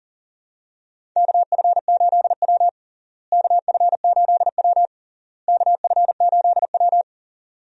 Morse code, heard as a keyed tone, about 40 wpm, 700 Hz. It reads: KF8W KF8W KF8W
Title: USA Call Signs - 4 Characters - Repeated 3x - 40wpm